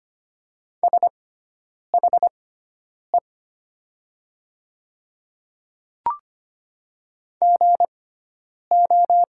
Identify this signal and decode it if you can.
Morse code, heard as a keyed tone, about 25 wpm, spaced as for 9 wpm, 700 Hz. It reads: SHE GO
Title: 6x Character Spacing - Top 100 Words - 25wpm